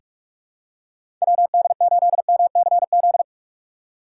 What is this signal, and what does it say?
Morse code, heard as a keyed tone, about 42 wpm, 700 Hz. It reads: WD8MCZ
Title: USA Call Signs - 6 Characters - 4x Character Spacing - 45wpm